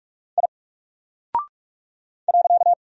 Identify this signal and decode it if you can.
Morse code, heard as a keyed tone, about 47 wpm, 700 Hz. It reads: I .
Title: Single Character (Letter:Number:Punctuation:Prosign) Alt 45wpm